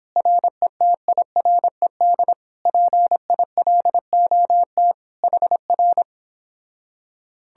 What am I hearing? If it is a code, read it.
Morse code, heard as a keyed tone, about 29 wpm, 700 Hz. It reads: RETIRED PILOT HR